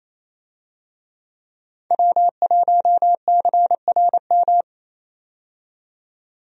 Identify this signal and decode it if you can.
Morse code, heard as a keyed tone, about 28 wpm, 700 Hz. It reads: W1CRM
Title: USA Call Signs - 5 Characters - 28wpm